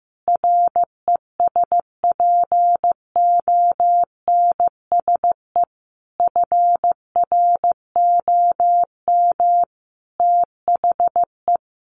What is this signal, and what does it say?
Morse code, heard as a keyed tone, about 15 wpm, 700 Hz. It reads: RESPONSE FROM THE
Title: Sentences from Top 1400 Words 15wpm